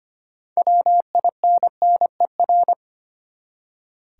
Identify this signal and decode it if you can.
Morse code, heard as a keyed tone, about 25 wpm, 700 Hz. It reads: WINNER
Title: Six Letter Words - 25wpm